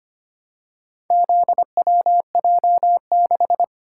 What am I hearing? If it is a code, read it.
Morse code, heard as a keyed tone, about 25 wpm, 700 Hz. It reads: ZWJ6
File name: practice / 25wpm / Four Letter-Number Combinations - 25wpm